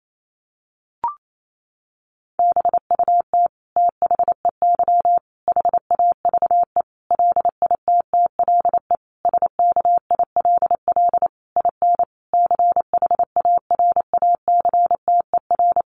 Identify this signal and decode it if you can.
Morse code, heard as a keyed tone, about 28 wpm, 700 Hz. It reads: BUT THEY HAVE LITTLE SKILL IN CHARACTER
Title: Sentences from Top 1200 Words 28wpm